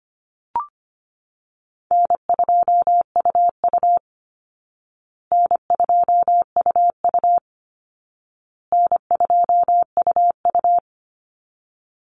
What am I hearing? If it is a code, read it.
Morse code, heard as a keyed tone, about 25 wpm, 700 Hz. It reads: N2UU N2UU N2UU
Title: Call Signs Repeated 2x - 25wpm